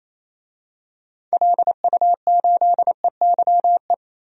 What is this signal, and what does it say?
Morse code, heard as a keyed tone, about 28 wpm, 700 Hz. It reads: LU8EYE
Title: DXCC-100 Call Signs - 6 Characters - Repeated 2x - 28wpm